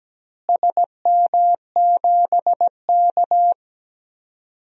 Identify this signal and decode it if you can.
Morse code, heard as a keyed tone, about 17 wpm, 700 Hz. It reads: SM7K
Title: DXCC-100 Call Signs - 4 Characters - Repeated 2x - 17wpm